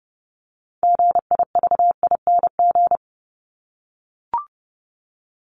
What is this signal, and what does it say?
Morse code, heard as a keyed tone, about 30 wpm, 700 Hz. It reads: GIVING
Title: ING Suffix - Top 100 Words with Base Words 30wpm